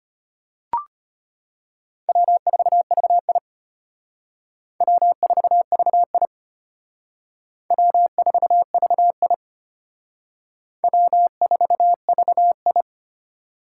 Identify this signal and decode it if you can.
Morse code, heard as a keyed tone, about 32 wpm, 700 Hz. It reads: W4VI W4VI W4VI W4VI